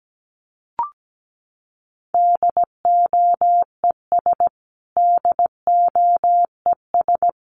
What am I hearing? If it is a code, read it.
Morse code, heard as a keyed tone, about 17 wpm, 700 Hz. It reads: DOES DOES
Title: Top 500 Words Repeated 17wpm